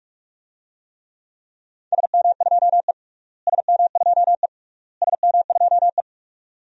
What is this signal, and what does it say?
Morse code, heard as a keyed tone, about 46 wpm, 700 Hz. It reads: SM2E SM2E SM2E
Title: DXCC-100 Call Signs - 4 Characters - Repeated 3x - 45wpm